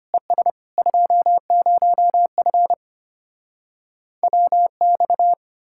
Morse code, heard as a keyed tone, about 33 wpm, 700 Hz. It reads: ES 20F WX